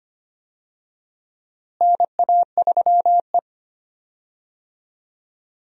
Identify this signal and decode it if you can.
Morse code, heard as a keyed tone, about 25 wpm, 700 Hz. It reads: NA3E
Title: USA Call Signs - 4 Characters - Repeated 3x - 25wpm